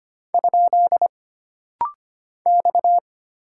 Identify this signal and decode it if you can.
Morse code, heard as a keyed tone, about 25 wpm, 700 Hz. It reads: ? X
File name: Single Character (Letter:Number:Punctuation:Prosign) - Rapid-Fire - 25wpm